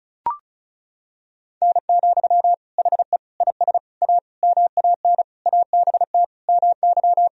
Morse code, heard as a keyed tone, about 36 wpm, 700 Hz. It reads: N, HE IS A MAN ABT MY
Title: Sentences from Top 400 Words and Abbreviations 35wpm